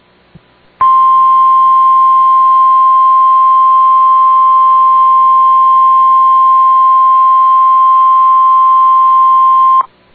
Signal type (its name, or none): Emergency Alert System (SAME) tones